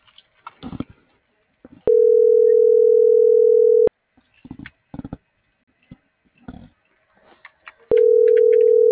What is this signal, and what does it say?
Ringback tone (North America), 440 + 480 Hz.